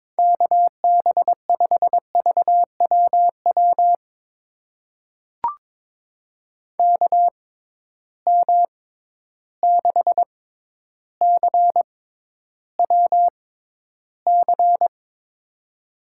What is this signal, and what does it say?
Morse code, heard as a keyed tone, about 22 wpm, spaced as for 11 wpm, 700 Hz. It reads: KB5VWW KM6CWC
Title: USA Call Signs - 6 Characters - 6x Character Spacing - 22wpm